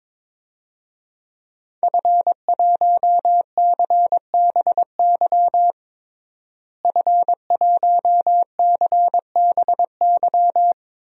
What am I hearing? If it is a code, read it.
Morse code, heard as a keyed tone, about 22 wpm, 700 Hz. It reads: F1CBY F1CBY
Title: DXCC-100 Call Signs - 5 Characters - Repeated 3x - 22wpm